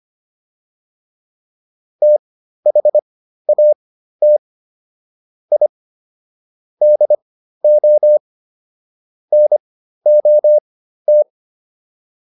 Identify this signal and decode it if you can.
Morse code, heard as a keyed tone, about 25 wpm, spaced as for 13 wpm, 600 Hz. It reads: THAT I DO NOT